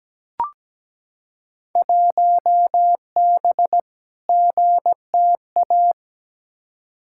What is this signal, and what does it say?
Morse code, heard as a keyed tone, about 17 wpm, 700 Hz. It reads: 1B GTA
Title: Field Day Exchanges Encore - 17wpm